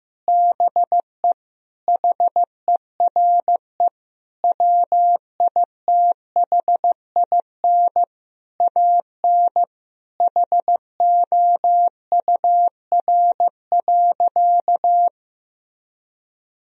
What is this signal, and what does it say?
Morse code, heard as a keyed tone, about 15 wpm, 700 Hz. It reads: BE HERE WITHIN AN HOUR.
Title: Sentences from Top 1000 Words - Review Part 4 - 15wpm